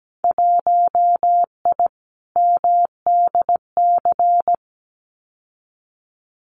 Morse code, heard as a keyed tone, about 17 wpm, 700 Hz. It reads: 1I MDC
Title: Winter Field Day Exchanges 17wpm